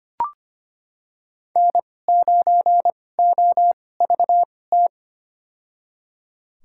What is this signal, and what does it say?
Morse code, heard as a keyed tone, about 25 wpm, spaced as for 18 wpm, 700 Hz. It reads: N9OVT
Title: USA Call Signs - 5 Characters - 2x Character Spacing - 25wpm